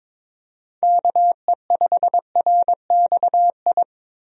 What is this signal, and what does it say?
Morse code, heard as a keyed tone, about 22 wpm, 700 Hz. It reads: KE5RXI
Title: USA Call Signs - 6 Characters - Repeated 3x - 22wpm